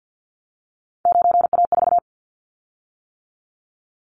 Morse code, heard as a keyed tone, about 50 wpm, 700 Hz. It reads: 9A4
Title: Callsign Prefixes for Top 100 DXCC - 50wpm